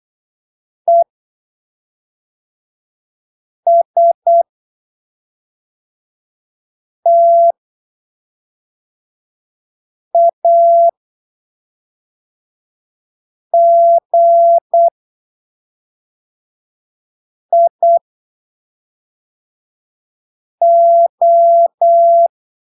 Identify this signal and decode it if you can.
Morse code, heard as a keyed tone, about 8 wpm, spaced as for 3 wpm, 680 Hz. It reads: ESTAGIO
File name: arquivos / texto / texto 05 08ppm